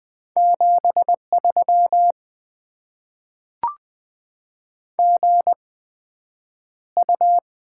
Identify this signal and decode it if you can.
Morse code, heard as a keyed tone, about 20 wpm, spaced as for 9 wpm, 700 Hz. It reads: 73 GU
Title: Top 20 QSO Elements - 8x Character Spacing - 20wpm